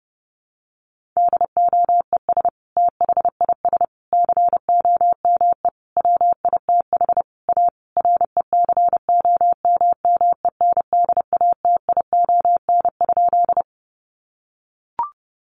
Morse code, heard as a keyed tone, about 28 wpm, 700 Hz. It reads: DOES THIS COME WITH A RECOMMENDATION?
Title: Sentences from Top 1400 Words 30wpm